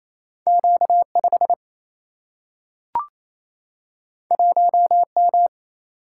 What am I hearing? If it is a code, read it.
Morse code, heard as a keyed tone, about 28 wpm, 700 Hz. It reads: Q 5 1 M
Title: Single-Letter Single-Number 28wpm